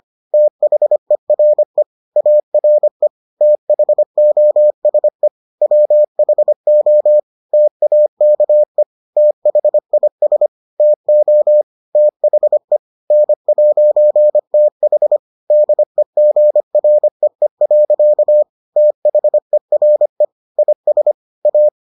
Morse code, heard as a keyed tone, about 25 wpm, 600 Hz. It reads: THERE ARE THOSE WHO TAKE THIS TO THE N'TH DEGREE. THERE IS A